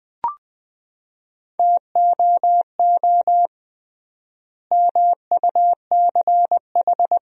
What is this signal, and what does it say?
Morse code, heard as a keyed tone, about 20 wpm, 700 Hz. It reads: TOO MUCH
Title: 3x Word-Spacing - Sentences from Top 200 Words - 20wpm